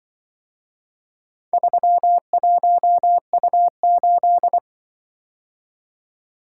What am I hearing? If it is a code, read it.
Morse code, heard as a keyed tone, about 24 wpm, 700 Hz. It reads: 31U8
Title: Speed-Racing - Four Letter-Number Combinations - 20wpm